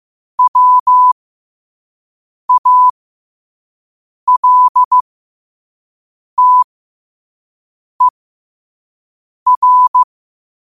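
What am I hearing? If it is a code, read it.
Morse code, heard as a keyed tone, about 15 wpm, spaced as for 5 wpm, 1000 Hz. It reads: WALTER
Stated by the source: Morse at 5 wpm.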